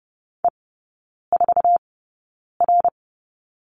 Morse code, heard as a keyed tone, about 30 wpm, spaced as for 9 wpm, 700 Hz. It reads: E4R
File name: course / 30:10wpm / 087 Lesson 15e - Number 5 - Three Characters 30wpm